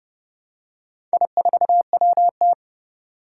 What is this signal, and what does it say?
Morse code, heard as a keyed tone, about 30 wpm, 700 Hz. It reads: I4WT
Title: Four Challenging Letter-Number Combinations 30wpm